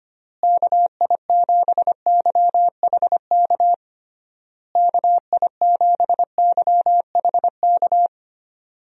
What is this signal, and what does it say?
Morse code, heard as a keyed tone, about 25 wpm, 700 Hz. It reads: KI7YHK KI7YHK
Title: USA Call Signs - 6 Characters - Repeated 3x - 25wpm